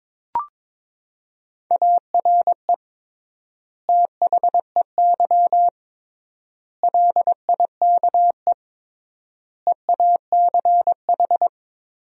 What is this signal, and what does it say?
Morse code, heard as a keyed tone, about 22 wpm, 700 Hz. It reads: ARE THEY LIKE EACH
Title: 3x Word-Spacing - Sentences from Top 200 Words - 22wpm